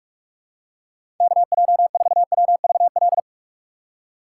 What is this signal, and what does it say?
Morse code, heard as a keyed tone, about 46 wpm, 700 Hz. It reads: KJ4WVL
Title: USA Call Signs - 6 Characters - 3x Character Spacing - 45wpm